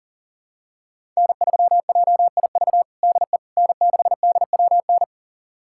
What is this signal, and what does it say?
Morse code, heard as a keyed tone, about 40 wpm, 700 Hz. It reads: N3JIV DE N6DWN
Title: USA Callsign DE Callsign - 5 Characters - 40wpm